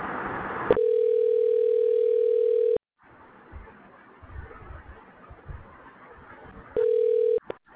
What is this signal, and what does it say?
Ringback tone (North America), 440 + 480 Hz.